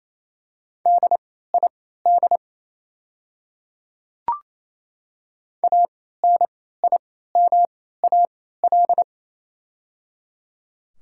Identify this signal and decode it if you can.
Morse code, heard as a keyed tone, about 28 wpm, spaced as for 16 wpm, 700 Hz. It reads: DID ANIMAL
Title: 3x Character Spacing - Top 300 Words - 28wpm